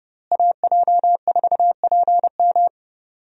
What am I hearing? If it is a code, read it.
Morse code, heard as a keyed tone, about 30 wpm, 700 Hz. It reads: AJ4PM
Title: USA Call Signs - 5 Characters - Repeated 3x - 30wpm